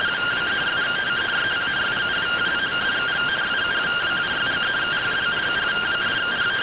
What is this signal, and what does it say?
Two-tone teleprinter signal, about 170 Hz shift.